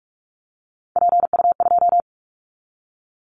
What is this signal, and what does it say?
Morse code, heard as a keyed tone, about 46 wpm, 700 Hz. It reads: PU2
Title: Callsign Prefixes for Top 50 DXCC - 45wpm